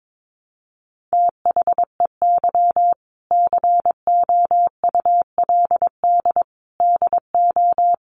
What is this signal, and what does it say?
Morse code, heard as a keyed tone, about 22 wpm, 700 Hz. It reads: THEY COULD DO
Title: Sentences from Top 900 Words 22wpm